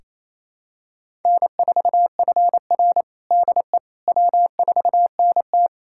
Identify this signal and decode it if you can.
Morse code, heard as a keyed tone, about 28 wpm, 700 Hz. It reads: N4FR DE W4NT
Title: USA Callsign DE Callsign - 4 Characters - 28wpm